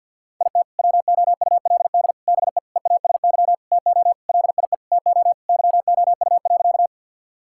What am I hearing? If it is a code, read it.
Morse code, heard as a keyed tone, about 50 wpm, 700 Hz. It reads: IT WOULD BE EASY TO LIE TO YOU.
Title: Sentences from Top 1600 Words 50wpm